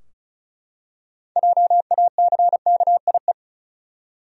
Morse code, heard as a keyed tone, about 36 wpm, 700 Hz. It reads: JACKIE
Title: USA Common Names - 35wpm